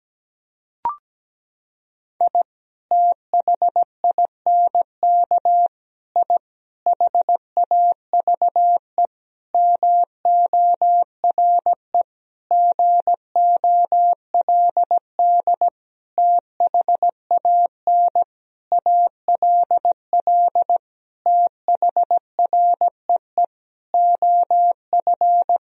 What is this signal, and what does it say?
Morse code, heard as a keyed tone, about 17 wpm, 700 Hz. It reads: I THINK I HAVE MORE GOLD THAN ALL THREE OF
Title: Sentences from Top 1000 Words - Review Part 3 - 17wpm